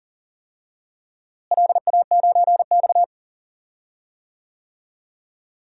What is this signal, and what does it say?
Morse code, heard as a keyed tone, about 40 wpm, 700 Hz. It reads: LA9X
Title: DXCC-100 Call Signs - 4 Characters - Repeated 2x - 40wpm